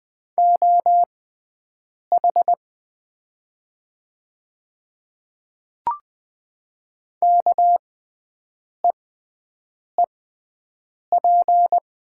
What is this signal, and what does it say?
Morse code, heard as a keyed tone, about 20 wpm, spaced as for 7 wpm, 700 Hz. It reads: OH KEEP
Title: 6x Character Spacing - Top 500 Words - 20wpm